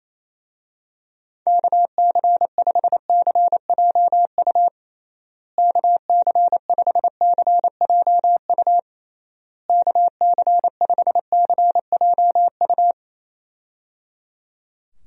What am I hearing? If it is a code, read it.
Morse code, heard as a keyed tone, about 28 wpm, 700 Hz. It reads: KC5CJU KC5CJU KC5CJU
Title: USA Call Signs - 6 Characters - Repeated 3x - 28wpm